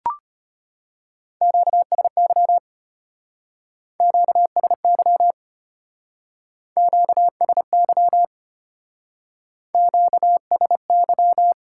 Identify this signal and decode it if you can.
Morse code, heard as a keyed tone, about 31 wpm, 700 Hz. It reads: QSY QSY QSY QSY